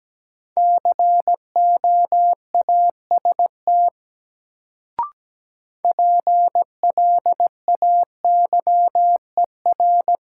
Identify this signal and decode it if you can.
Morse code, heard as a keyed tone, about 17 wpm, 700 Hz. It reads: COAST PLAYER
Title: Sentences from Top 1000 Words - Review Part 4 - New Word Drill - 17wpm